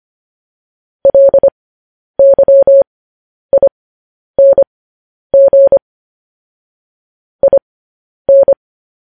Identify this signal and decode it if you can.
Morse code, heard as a keyed tone, about 25 wpm, spaced as for 10 wpm, 550 Hz. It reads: LYING IN
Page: news headlines in morse code at 10 wpm